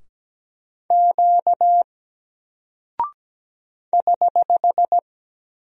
Morse code, heard as a keyed tone, about 17 wpm, 700 Hz. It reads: Q <HH>
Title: Single Character (Letter:Number:Punctuation:Prosign) Alt 17wpm